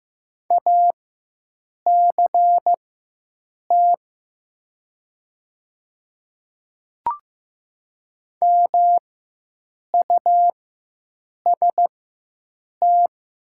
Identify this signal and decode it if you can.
Morse code, heard as a keyed tone, about 15 wpm, spaced as for 7 wpm, 700 Hz. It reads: ACT MUST
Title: 4x Character Spacing - Top 200 Words - 15wpm